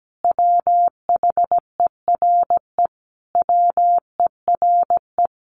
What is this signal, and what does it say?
Morse code, heard as a keyed tone, about 17 wpm, 700 Hz. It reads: WHERE WERE
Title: Sentences from Top 1000 Words 17wpm